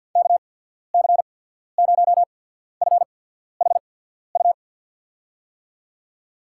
Morse code, heard as a keyed tone, about 51 wpm, spaced as for 14 wpm, 700 Hz. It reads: KC0FHU
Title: USA Call Signs - 6 Characters - 8x Character Spacing - 50wpm